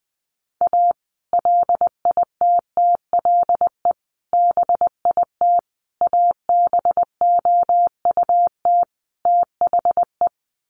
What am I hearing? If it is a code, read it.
Morse code, heard as a keyed tone, about 19 wpm, 700 Hz. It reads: A LITTLE BIT ABOUT THE
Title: Sets of 5 Words Encore2 20wpm